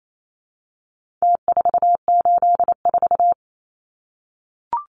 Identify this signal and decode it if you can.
Morse code, heard as a keyed tone, about 28 wpm, 700 Hz. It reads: T484
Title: Numbers - Contest Style - 4-Digit 28wpm